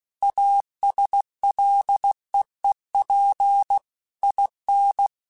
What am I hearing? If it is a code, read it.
Morse code, heard as a keyed tone, about 16 wpm, 800 Hz. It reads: ASLEEP IN